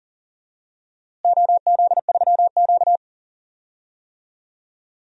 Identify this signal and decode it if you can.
Morse code, heard as a keyed tone, about 40 wpm, 700 Hz. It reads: OZ3Q
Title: DXCC-100 Call Signs - 4 Characters - Repeated 3x - 40wpm